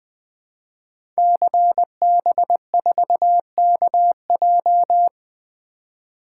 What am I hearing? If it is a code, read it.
Morse code, heard as a keyed tone, about 20 wpm, 700 Hz. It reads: CB4KJ
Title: Code Groups - 5 Letter:Numbers - 20wpm